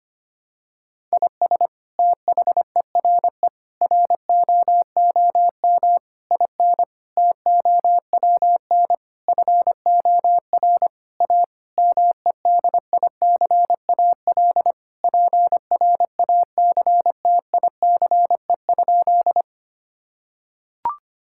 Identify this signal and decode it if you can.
Morse code, heard as a keyed tone, about 25 wpm, 700 Hz. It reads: IS THERE ROOM IN TOWN FOR A MEDICAL PRACTICE?
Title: Sentences from Top 1000 Words - Review Part 5 - 25wpm